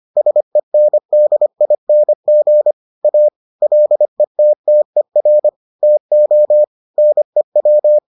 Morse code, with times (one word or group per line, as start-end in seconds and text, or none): 0.2-2.7 SENDING
3.0-3.3 A
3.6-5.5 LETTER
5.8-6.6 TO
7.0-8.0 NEW